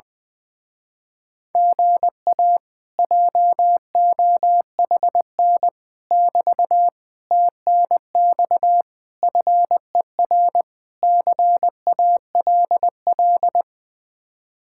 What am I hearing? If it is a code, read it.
Morse code, heard as a keyed tone, about 20 wpm, 700 Hz. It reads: GA JOHN <BT> TNX FER CALL
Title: Speed-Racing Common QSO Phrases - 20wpm